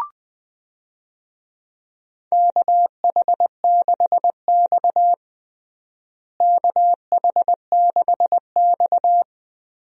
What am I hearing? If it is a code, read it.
Morse code, heard as a keyed tone, about 21 wpm, 700 Hz. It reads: KH6X KH6X
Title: DXCC-100 Call Signs - 4 Characters - Repeated 2x - 20wpm